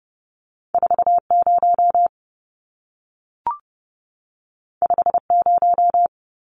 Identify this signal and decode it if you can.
Morse code, heard as a keyed tone, about 30 wpm, 700 Hz. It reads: 40 50
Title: Numbers - 2-Digit 30wpm